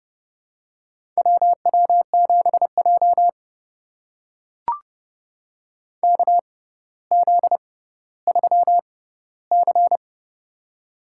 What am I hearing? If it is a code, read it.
Morse code, heard as a keyed tone, about 32 wpm, spaced as for 16 wpm, 690 Hz. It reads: WW7J KZ3C
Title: USA Call Signs - 4 Characters - 6x Character Spacing - 30wpm